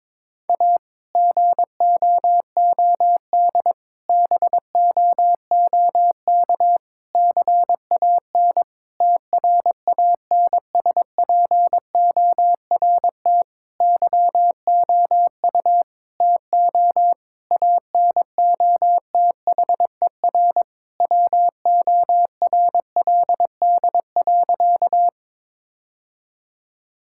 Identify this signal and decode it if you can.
Morse code, heard as a keyed tone, about 22 wpm, 700 Hz. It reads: A GOOD BOOK CAN TRANSPORT YOU TO ANOTHER WORLD.
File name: Difficult Sentences with Hints - 22wpm